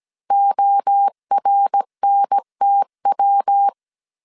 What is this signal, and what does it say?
Morse code, heard as a keyed tone, about 17 wpm, 800 Hz. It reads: ORNTW